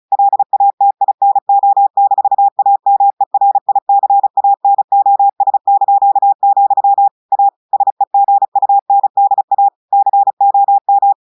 Morse code, heard as a keyed tone, about 36 wpm, 800 Hz. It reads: LATINO-AMERICANOS), A SEGUNDA COM
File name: audios / portugais / portugais texte en clair 1 vitesse 35 mots minutes